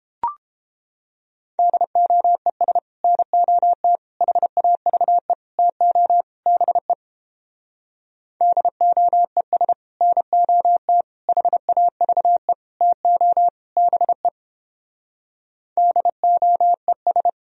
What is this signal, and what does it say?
Morse code, heard as a keyed tone, about 29 wpm, 700 Hz. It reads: DOES NOT HAVE TO BE DOES NOT HAVE TO BE DOES